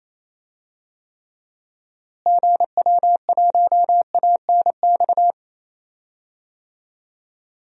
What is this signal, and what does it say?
Morse code, heard as a keyed tone, about 28 wpm, 700 Hz. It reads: GW1ANX
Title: DXCC-100 Call Signs - 6 Characters - Repeated 3x - 28wpm